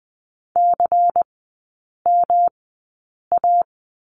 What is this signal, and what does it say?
Morse code, heard as a keyed tone, about 20 wpm, spaced as for 8 wpm, 700 Hz. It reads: CMA
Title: 114 Lesson 19e - Letter w - Three Characters 20wpm